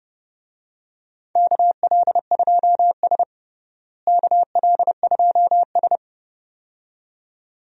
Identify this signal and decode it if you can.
Morse code, heard as a keyed tone, about 32 wpm, 700 Hz. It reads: KL2S KL2S
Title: USA Call Signs - 4 Characters - Repeated 2x - 30wpm